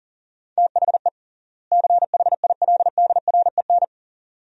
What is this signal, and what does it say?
Morse code, heard as a keyed tone, about 40 wpm, 700 Hz. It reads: THE CHILDREN
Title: 3x Word-Spacing - Sentences from Top 300 Words - 40wpm